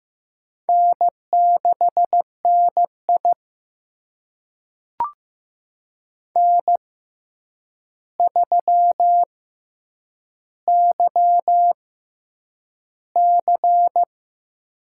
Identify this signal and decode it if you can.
Morse code, heard as a keyed tone, about 15 wpm, spaced as for 8 wpm, 700 Hz. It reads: N6NI N3YC
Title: USA Call Signs - 4 Characters - 6x Character Spacing - 15wpm